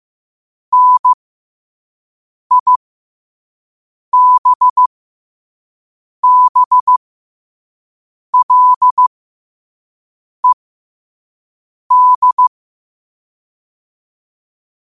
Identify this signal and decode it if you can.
Morse code, heard as a keyed tone, about 15 wpm, spaced as for 5 wpm, 1000 Hz. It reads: NIBBLED
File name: qotdfiles / 5wpm